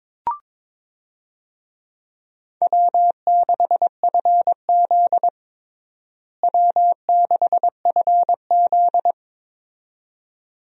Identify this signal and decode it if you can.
Morse code, heard as a keyed tone, about 22 wpm, 700 Hz. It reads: W6FZ W6FZ
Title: USA Call Signs - 4 Characters - Repeated 2x - 22wpm